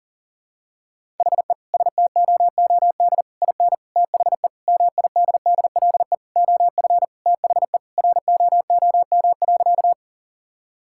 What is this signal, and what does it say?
Morse code, heard as a keyed tone, about 40 wpm, 700 Hz. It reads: HE STOOD IN THE MIDDLE OF THE ROOM.
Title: Sentences from Top 1000 Words - Review Part 6 - 40wpm